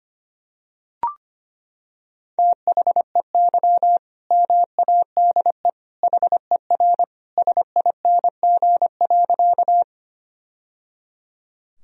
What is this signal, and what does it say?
Morse code, heard as a keyed tone, about 25 wpm, 700 Hz. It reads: THEY MADE HER SING.
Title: Sentences from Top 1000 Words - Review Part 2 - 25wpm